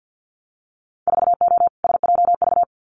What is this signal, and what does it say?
Morse code, heard as a keyed tone, about 50 wpm, 700 Hz. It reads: 4O SJV
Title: Winter Field Day Exchanges Encore 50wpm